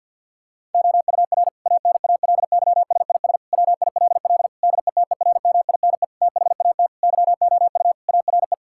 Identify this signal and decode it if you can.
Morse code, heard as a keyed tone, about 50 wpm, 700 Hz. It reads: OUR ANALYSIS WILL DETERMINE THAT YOU ARE
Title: Sentences from Top 1000 Words - Review Part 5 - 50wpm